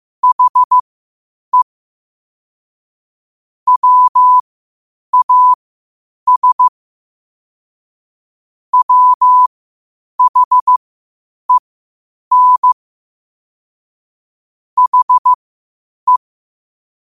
Morse code, heard as a keyed tone, about 15 wpm, spaced as for 8 wpm, 1000 Hz. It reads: HE WAS WHEN HE
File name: qotdfiles / 8wpm